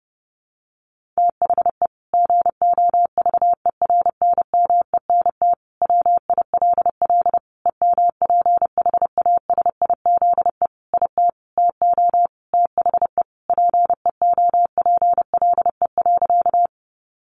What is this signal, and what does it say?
Morse code, heard as a keyed tone, about 30 wpm, 700 Hz. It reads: THE GOVERNMENT WILL EMPHASIZE IT TO THE PEOPLE.
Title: Sentences from Top 1500 Words 30wpm